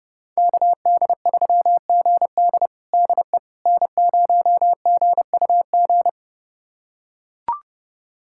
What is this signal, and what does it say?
Morse code, heard as a keyed tone, about 30 wpm, 700 Hz. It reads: KD3GD DE N0GUG
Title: USA Callsign DE Callsign - 5 Characters - 30wpm